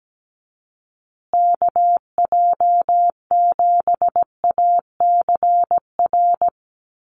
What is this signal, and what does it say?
Morse code, heard as a keyed tone, about 17 wpm, 700 Hz. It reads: KJ7ACR
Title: USA Call Signs Repeated 2x 17wpm